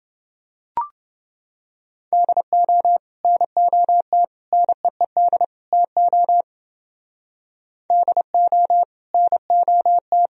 Morse code, heard as a keyed tone, about 28 wpm, 700 Hz. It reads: DO NOT NEED TO DO NOT